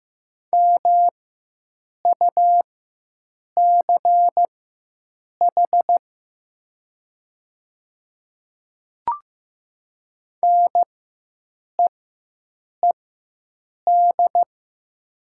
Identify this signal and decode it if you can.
Morse code, heard as a keyed tone, about 15 wpm, spaced as for 7 wpm, 700 Hz. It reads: MUCH NEED
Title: 4x Character Spacing - Top 200 Words - 15wpm